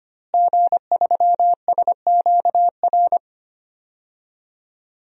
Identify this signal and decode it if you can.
Morse code, heard as a keyed tone, about 25 wpm, 700 Hz. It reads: G3SQR
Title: DXCC-100 Call Signs - 5 Characters - Repeated 2x - 25wpm